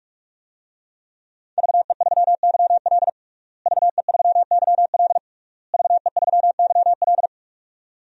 Morse code, heard as a keyed tone, about 46 wpm, 700 Hz. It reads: VE3YL VE3YL VE3YL
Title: DXCC-100 Call Signs - 5 Characters - Repeated 3x - 45wpm